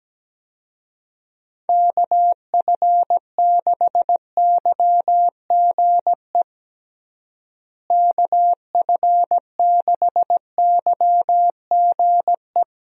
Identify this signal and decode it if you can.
Morse code, heard as a keyed tone, about 17 wpm, 700 Hz. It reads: KF6YGE KF6YGE
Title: USA Call Signs - 6 Characters - Repeated 2x - 17wpm